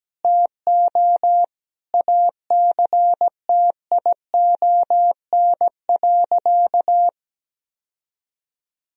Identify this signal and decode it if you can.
Morse code, heard as a keyed tone, about 17 wpm, 700 Hz. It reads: TO ACTION.
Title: Sentences from Top 1000 Words - Review Part 4 - 17wpm